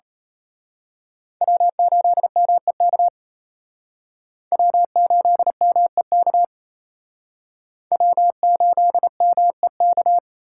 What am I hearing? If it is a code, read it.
Morse code, heard as a keyed tone, about 32 wpm, 700 Hz. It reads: W8MEK W8MEK W8MEK
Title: Speed Racing - USA 1x3 Call Signs 28wpm